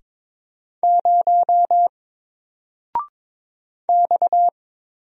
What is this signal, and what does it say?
Morse code, heard as a keyed tone, about 22 wpm, 700 Hz. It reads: 0 X
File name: Single-Letter-Number-22wpm